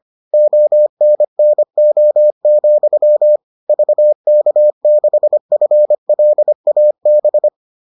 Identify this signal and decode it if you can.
Morse code, heard as a keyed tone, about 25 wpm, 600 Hz. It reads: ONNO, VK6FLAB